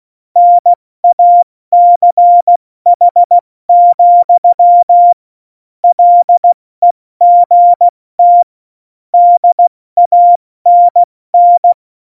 Morse code, heard as a keyed tone, about 16 wpm, 700 Hz. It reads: NACH, LEGT DANN
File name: Morsekurs / Witze / Witz 15. Oktober 2019, 70 F80